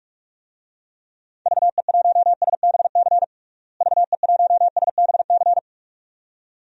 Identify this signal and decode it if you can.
Morse code, heard as a keyed tone, about 46 wpm, 700 Hz. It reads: VE1SBC VE1SBC
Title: DXCC-100 Call Signs - 6 Characters - Repeated 2x - 45wpm